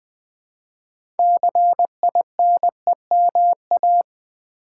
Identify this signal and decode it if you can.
Morse code, heard as a keyed tone, about 20 wpm, 700 Hz. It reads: CINEMA
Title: Build A Word - 2 Letters At A Time - 6-Letter Words - 20wpm